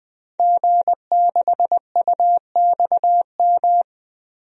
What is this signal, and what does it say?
Morse code, heard as a keyed tone, about 20 wpm, 700 Hz. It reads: G6UXM
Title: DXCC-100 Call Signs - 5 Characters - 20wpm